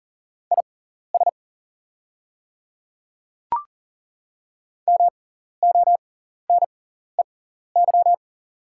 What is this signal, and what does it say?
Morse code, heard as a keyed tone, about 40 wpm, spaced as for 14 wpm, 700 Hz. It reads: IS MONEY
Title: 6x Character Spacing - Top 500 Words - 40wpm